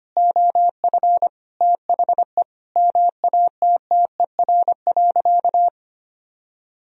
Morse code, heard as a keyed tone, about 25 wpm, 700 Hz. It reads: OF THE MATTER.
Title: Sentences from Top 1000 Words - Review Part 3 - 25wpm